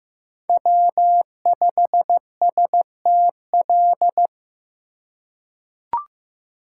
Morse code, heard as a keyed tone, about 15 wpm, 700 Hz. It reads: W5STL
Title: USA Call Signs - 5 Characters - 2x Character Spacing - 15wpm